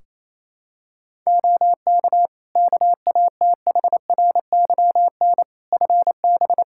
Morse code, heard as a keyed tone, about 28 wpm, 700 Hz. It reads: OK KATHRYN FB
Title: Practice QSO - Exchange 2 - Info - 28wpm